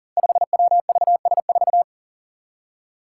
Morse code, heard as a keyed tone, about 40 wpm, 700 Hz. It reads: 5WVS4
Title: Five Challenging Letter-Number Combinations 40wpm